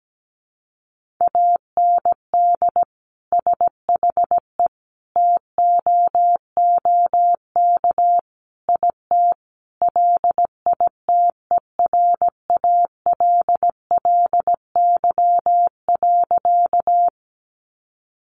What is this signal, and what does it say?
Morse code, heard as a keyed tone, about 17 wpm, 700 Hz. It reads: AND SHE TOOK IT LITERALLY.
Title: Sentences from Top 1400 Words 17wpm